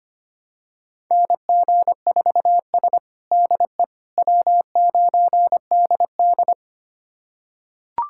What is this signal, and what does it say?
Morse code, heard as a keyed tone, about 25 wpm, 700 Hz. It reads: NG4S DE W9DD
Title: USA Callsign DE Callsign - 4 Characters - 25wpm